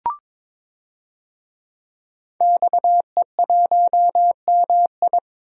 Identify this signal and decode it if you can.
Morse code, heard as a keyed tone, about 21 wpm, 700 Hz. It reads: XE1MI
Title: DXCC-100 Call Signs - 5 Characters - Repeated 2x - 22wpm